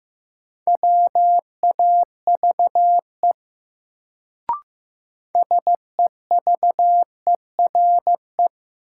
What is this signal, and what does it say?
Morse code, heard as a keyed tone, about 15 wpm, 700 Hz. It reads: WAVE SEVERE
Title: Sentences from Top 1400 Words - New Word Drill - 15wpm